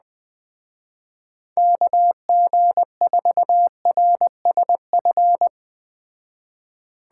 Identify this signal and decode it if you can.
Morse code, heard as a keyed tone, about 20 wpm, 700 Hz. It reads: KG4RSF
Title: Speed Racing - USA 2x3 Call Signs 17wpm